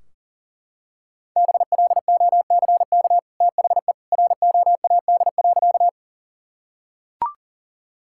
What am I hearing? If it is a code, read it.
Morse code, heard as a keyed tone, about 40 wpm, 700 Hz. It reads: BLOCK THE ROAD.
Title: Sentences from Top 1600 Words 40wpm